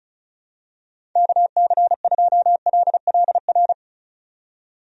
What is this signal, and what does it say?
Morse code, heard as a keyed tone, about 34 wpm, 700 Hz. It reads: KC2LLR
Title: USA Call Signs - 6 Characters - 35wpm